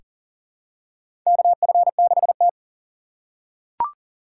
Morse code, heard as a keyed tone, about 40 wpm, 700 Hz. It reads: KF6T
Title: K1USN SST QSO Practice - 40wpm